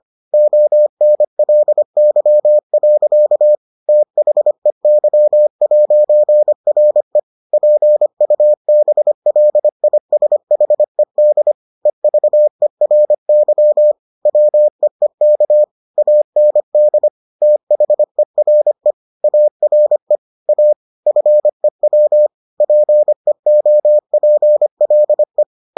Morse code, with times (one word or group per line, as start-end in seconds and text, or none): 0.3-3.5 ONLY.
3.9-7.2 THEY'RE
7.5-11.5 PUBLISHED
11.9-13.9 EVERY
14.3-15.6 WEEK
16.0-17.1 AND
17.4-18.9 THERE
19.2-20.2 ARE
20.5-20.7 A
21.1-22.3 FEW
22.6-25.4 PEOPLE